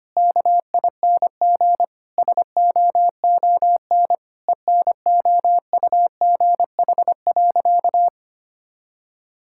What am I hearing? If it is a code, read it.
Morse code, heard as a keyed tone, about 25 wpm, 700 Hz. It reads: KING SOON ENOUGH.
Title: Sentences from Top 1000 Words - Review Part 6 - 25wpm